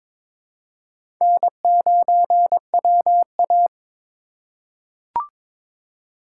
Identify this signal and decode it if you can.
Morse code, heard as a keyed tone, about 22 wpm, 700 Hz. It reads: N9WA
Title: USA Call Signs - 4 Characters - 4x Character Spacing - 22wpm